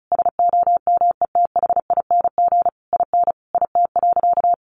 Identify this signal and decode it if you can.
Morse code, heard as a keyed tone, about 34 wpm, 700 Hz. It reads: SOMETHING IN IT.
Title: Sentences from Top 1100 Words 35wpm